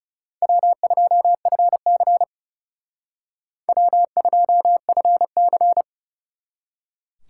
Morse code, heard as a keyed tone, about 34 wpm, 700 Hz. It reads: W2FC W2FC
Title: Speed Racing - USA 1x2 Call Signs 30wpm